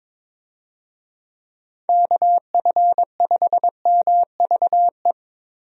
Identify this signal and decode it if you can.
Morse code, heard as a keyed tone, about 22 wpm, 700 Hz. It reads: KF5MVE
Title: USA Call Signs - 6 Characters - Repeated 3x - 22wpm